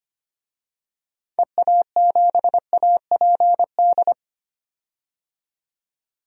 Morse code, heard as a keyed tone, about 25 wpm, 700 Hz. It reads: EA7APD
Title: DXCC-100 Call Signs - 6 Characters - 25wpm